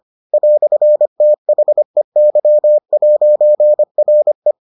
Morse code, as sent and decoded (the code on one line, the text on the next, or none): .-..-. - .... -.-- .----. .-. .
"THEY'RE